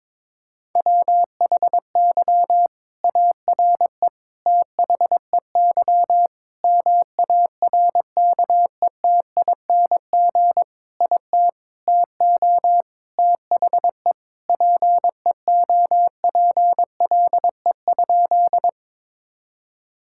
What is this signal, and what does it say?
Morse code, heard as a keyed tone, about 22 wpm, 700 Hz. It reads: WHY ARE THEY MARKETING IT TO THE PEOPLE?
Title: Sentences from Top 1000 Words - Review Part 5 - 22wpm